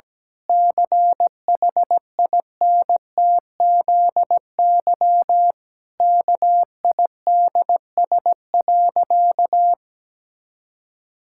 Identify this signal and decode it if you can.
Morse code, heard as a keyed tone, about 17 wpm, 700 Hz. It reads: CHINTZY KIDS.
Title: Pangram Challenge v2 - 17wpm